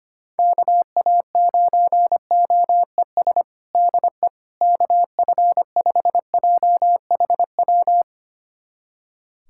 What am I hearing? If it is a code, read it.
Morse code, heard as a keyed tone, about 25 wpm, 700 Hz. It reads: KA9OES DE KF5JHW
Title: USA Callsign DE Callsign 25wpm